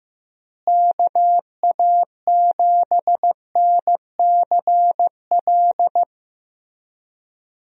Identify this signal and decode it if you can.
Morse code, heard as a keyed tone, about 15 wpm, 700 Hz. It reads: KA7NCL
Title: USA Call Signs - 6 Characters - Repeated 3x - 15wpm